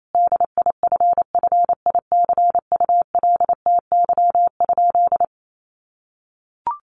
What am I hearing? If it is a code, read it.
Morse code, heard as a keyed tone, about 28 wpm, 700 Hz. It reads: DIFFICULTY?
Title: Sentences from Top 1500 Words 28wpm